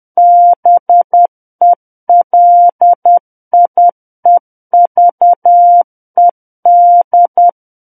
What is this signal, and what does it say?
Morse code, heard as a keyed tone, about 10 wpm, 700 Hz. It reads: BELIEVED
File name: Book One / 01-Chapter 1 - 10 WPM